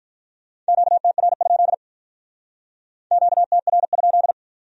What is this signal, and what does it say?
Morse code, heard as a keyed tone, about 46 wpm, 700 Hz. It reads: QTR? QTR?